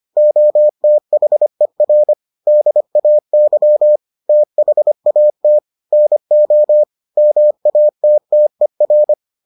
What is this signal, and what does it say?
Morse code, heard as a keyed tone, about 25 wpm, 600 Hz. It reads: OTHER DAY THAT NO MATTER